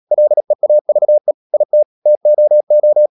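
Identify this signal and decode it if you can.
Morse code, heard as a keyed tone, about 37 wpm, 600 Hz. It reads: LEAVE IT TOO
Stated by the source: Morse code practice at 35–39 wpm sending English text of Ken Follett's novel Never, from CW med Gnister.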